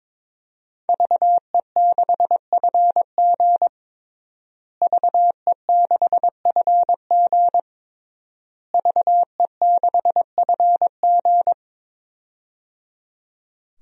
Morse code, heard as a keyed tone, about 22 wpm, 700 Hz. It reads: VE6FG VE6FG VE6FG
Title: DXCC-100 Call Signs - 5 Characters - Repeated 3x - 22wpm